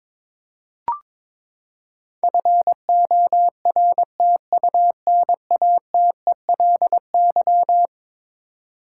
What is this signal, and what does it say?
Morse code, heard as a keyed tone, about 22 wpm, 700 Hz. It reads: FORTUNATELY
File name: practice / 22wpm / Eleven Letter Words - 22wpm